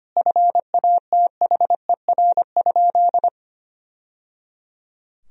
Morse code, heard as a keyed tone, about 25 wpm, 700 Hz. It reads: FATHER?
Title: Sentences from Top 1000 Words - Review Part 2 - 25wpm